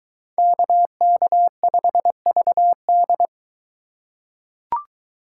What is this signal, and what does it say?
Morse code, heard as a keyed tone, about 22 wpm, 700 Hz. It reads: KK5VD